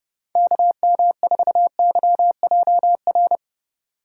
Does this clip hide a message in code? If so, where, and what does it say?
Yes: Morse code from 0.4 to 3.4 s, heard as a keyed tone, about 28 wpm, 700 Hz. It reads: KM4YJR